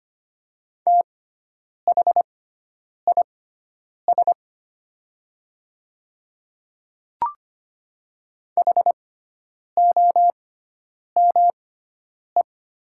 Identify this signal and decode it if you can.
Morse code, heard as a keyed tone, about 25 wpm, spaced as for 9 wpm, 700 Hz. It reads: THIS HOME
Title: 6x Character Spacing - Top 200 Words - 25wpm